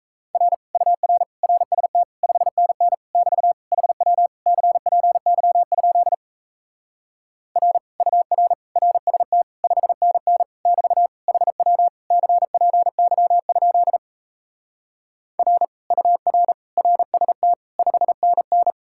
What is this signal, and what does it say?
Morse code, heard as a keyed tone, about 38 wpm, 700 Hz. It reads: R UR RST 5NN <BT> HW CPY? R UR RST 5NN <BT> HW CPY? R UR RST 5NN